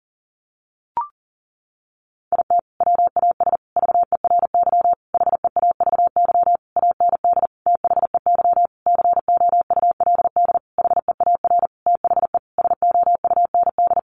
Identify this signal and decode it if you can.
Morse code, heard as a keyed tone, about 40 wpm, 700 Hz. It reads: IT WAS VERY HEAVY AND THEY COULD HEAR THE SOUND
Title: Sentences from Top 1100 Words 40wpm